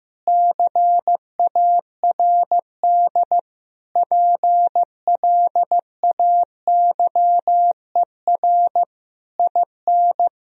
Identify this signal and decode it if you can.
Morse code, heard as a keyed tone, about 15 wpm, 700 Hz. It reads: CARD PLAYER IN
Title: Sentences from Top 1000 Words - Review Part 4 - 15wpm